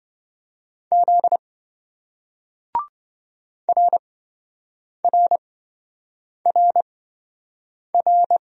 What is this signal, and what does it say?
Morse code, heard as a keyed tone, about 26 wpm, spaced as for 7 wpm, 700 Hz. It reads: Z RRRR